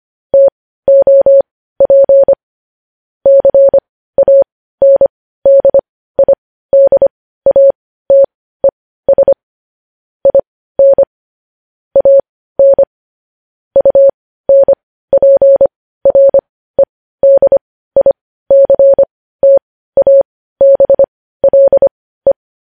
Morse code, heard as a keyed tone, about 25 wpm, spaced as for 15 wpm, 550 Hz. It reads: TOP CANDIDATES IN AN UNPREDICTABLE
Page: news headlines in morse code at 15 wpm